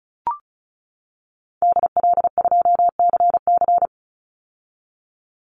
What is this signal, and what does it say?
Morse code, heard as a keyed tone, about 34 wpm, 700 Hz. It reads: DL2CC
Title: Call Signs - 35wpm